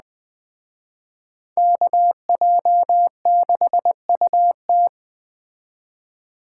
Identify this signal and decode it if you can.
Morse code, heard as a keyed tone, about 20 wpm, 700 Hz. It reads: KJ6UT
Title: Speed Racing - USA 2x2 Call Signs 17wpm